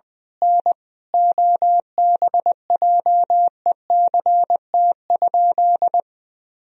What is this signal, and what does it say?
Morse code, heard as a keyed tone, about 21 wpm, 700 Hz. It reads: N OBJECT?
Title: Sentences from Top 500 Words and Abbreviations 20wpm